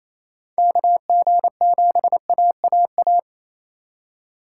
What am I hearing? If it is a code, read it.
Morse code, heard as a keyed tone, about 28 wpm, 700 Hz. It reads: KG7AAA
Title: USA Call Signs - 6 Characters - 3x Character Spacing - 28wpm